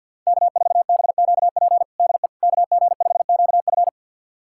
Morse code, heard as a keyed tone, about 48 wpm, 700 Hz. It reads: K4BQP DE KG5QF
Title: USA Callsign DE Callsign 50wpm